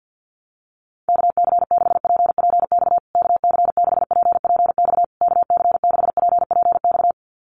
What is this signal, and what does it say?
Morse code, heard as a keyed tone, about 50 wpm, 700 Hz. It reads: KC6PPX KC6PPX KC6PPX
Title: USA Call Signs Repeated 3x 50wpm